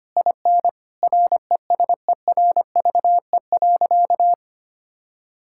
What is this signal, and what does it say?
Morse code, heard as a keyed tone, about 25 wpm, 700 Hz. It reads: IN RESERVE.
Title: Sentences from Top 1000 Words - Review Part 6 - 25wpm